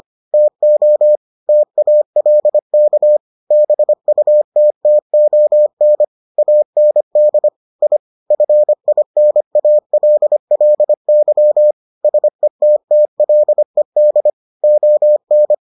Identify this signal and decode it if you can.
Morse code, heard as a keyed tone, about 25 wpm, 600 Hz. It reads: TO TALK BUTTON AND I FINALLY SETTLED ON